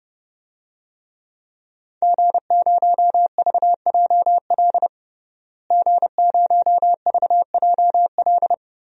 Morse code, heard as a keyed tone, about 30 wpm, 700 Hz. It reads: G0VJL G0VJL
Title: DXCC-100 Call Signs - 5 Characters - Repeated 3x - 30wpm